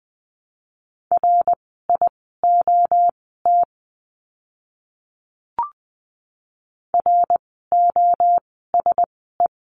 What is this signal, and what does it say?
Morse code, heard as a keyed tone, about 20 wpm, spaced as for 15 wpm, 700 Hz. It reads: RIOT ROSE
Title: 061 Lesson 11f - Letter h - Words 20wpm